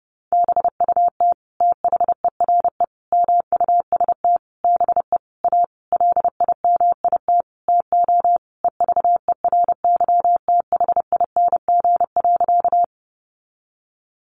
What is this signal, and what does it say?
Morse code, heard as a keyed tone, about 30 wpm, 700 Hz. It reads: BUT THERE MUST BE A LIMIT TO EVERYTHING.
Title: Sentences from Top 1500 Words 30wpm